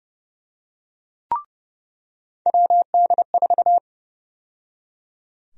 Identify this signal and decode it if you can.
Morse code, heard as a keyed tone, about 31 wpm, 700 Hz. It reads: WD4
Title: USA Call Sign Prefixes - 30wpm